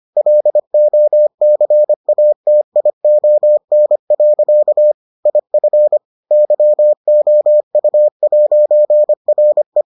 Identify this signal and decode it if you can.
Morse code, heard as a keyed tone, about 25 wpm, 600 Hz. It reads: LOCATION. IF YOU'RE